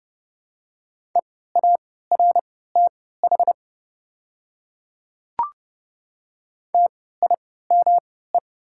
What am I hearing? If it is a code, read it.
Morse code, heard as a keyed tone, about 32 wpm, spaced as for 17 wpm, 690 Hz. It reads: EARTH TIME
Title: 3x Character Spacing - Top 200 Words - 30wpm